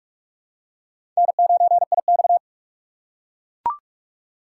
Morse code, heard as a keyed tone, about 43 wpm, 700 Hz. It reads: N9IX
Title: USA Call Signs - 4 Characters - 2x Character Spacing - 45wpm